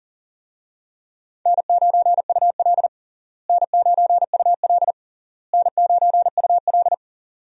Morse code, heard as a keyed tone, about 40 wpm, 700 Hz. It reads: N9UL N9UL N9UL
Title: USA Call Signs - 4 Characters - Repeated 3x - 40wpm